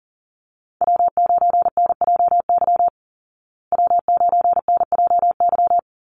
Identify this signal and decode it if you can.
Morse code, heard as a keyed tone, about 40 wpm, 700 Hz. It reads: W9NJY W9NJY
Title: Call Signs Repeated 3x - 40wpm